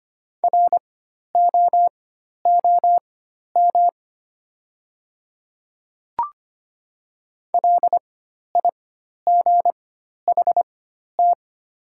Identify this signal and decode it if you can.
Morse code, heard as a keyed tone, about 25 wpm, spaced as for 12 wpm, 700 Hz. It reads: ROOM LIGHT
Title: 4x Character Spacing - Top 300 Words - 25wpm